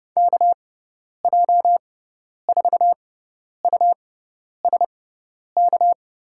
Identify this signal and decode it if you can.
Morse code, heard as a keyed tone, about 31 wpm, spaced as for 10 wpm, 700 Hz. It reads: KJ4USK
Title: USA Call Signs - 6 Characters - 6x Character Spacing - 30wpm